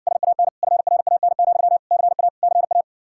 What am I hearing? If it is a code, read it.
Morse code, heard as a keyed tone, about 60 wpm, 700 Hz. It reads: SAN FRAN, CA CA